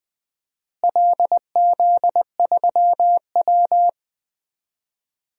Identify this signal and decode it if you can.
Morse code, heard as a keyed tone, about 20 wpm, 700 Hz. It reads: LZ3W
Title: DXCC-100 Call Signs - 4 Characters - Repeated 2x - 20wpm